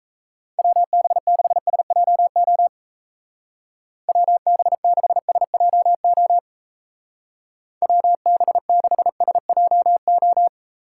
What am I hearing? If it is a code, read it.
Morse code, heard as a keyed tone, about 34 wpm, 700 Hz. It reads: WB6SJO WB6SJO WB6SJO
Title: Speed Racing - USA Call Signs 28wpm